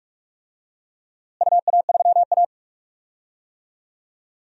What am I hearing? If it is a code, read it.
Morse code, heard as a keyed tone, about 44 wpm, 700 Hz. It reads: UA3A
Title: DXCC-100 Call Signs - 4 Characters - Repeated 2x - 45wpm